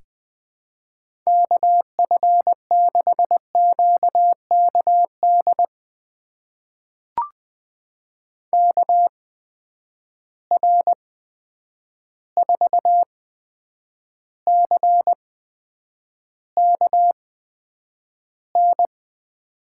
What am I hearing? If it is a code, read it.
Morse code, heard as a keyed tone, about 20 wpm, spaced as for 9 wpm, 700 Hz. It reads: KF6QKD KR4CKN
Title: USA Call Signs - 6 Characters - 8x Character Spacing - 20wpm